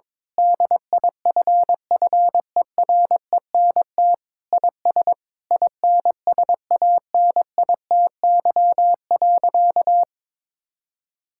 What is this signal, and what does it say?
Morse code, heard as a keyed tone, about 22 wpm, 700 Hz. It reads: DIFFERENT IS INSANITY.
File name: practice / 22wpm / Sentences with 8-Word Phrases - 22wpm